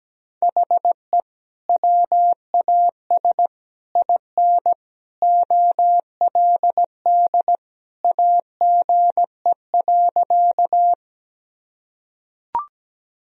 Morse code, heard as a keyed tone, about 17 wpm, 700 Hz. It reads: HE WAS IN OLD AGE.
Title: Sentences from Top 1000 Words - Review Part 3 - 17wpm